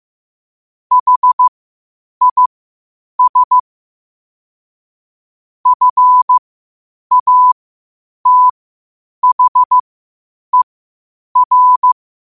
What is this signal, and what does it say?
Morse code, heard as a keyed tone, about 15 wpm, spaced as for 8 wpm, 1000 Hz. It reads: HIS FATHER
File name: qotdfiles / 8wpm